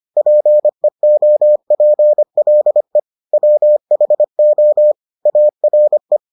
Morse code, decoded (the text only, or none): PEOPLE WHO ARE